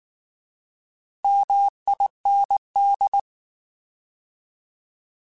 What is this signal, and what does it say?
Morse code, heard as a keyed tone, about 19 wpm, 800 Hz. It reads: MIND
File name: Alphabetical / Alphabetical mp3s / Ms @18